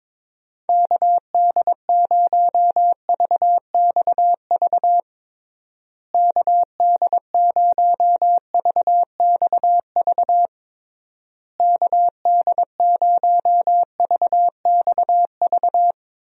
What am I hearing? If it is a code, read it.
Morse code, heard as a keyed tone, about 21 wpm, 700 Hz. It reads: KD0VXV KD0VXV KD0VXV
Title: USA Call Signs - 6 Characters - Repeated 3x - 22wpm